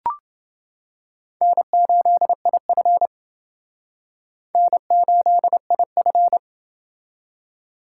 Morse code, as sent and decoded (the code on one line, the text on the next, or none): -. ---.. .. ..-. / -. ---.. .. ..-.
N8IF N8IF